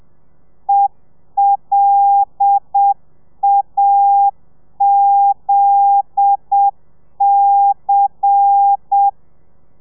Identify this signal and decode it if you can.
Morse code, heard as a keyed tone, about 7 wpm, 800 Hz. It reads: ELAZC
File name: telegrafia / i2tfj / Lez10 30